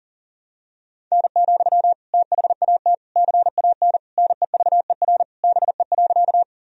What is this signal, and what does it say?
Morse code, heard as a keyed tone, about 40 wpm, 700 Hz. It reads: N, THAT CAN NEVER BE.
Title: Sentences from Top 300 Words and Abbreviations 40wpm